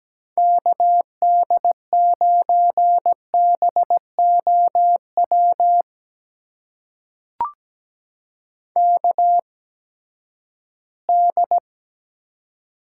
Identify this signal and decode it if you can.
Morse code, heard as a keyed tone, about 17 wpm, spaced as for 12 wpm, 700 Hz. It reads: KD9BOW KD
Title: USA Call Signs - 6 Characters - 8x Character Spacing - 17wpm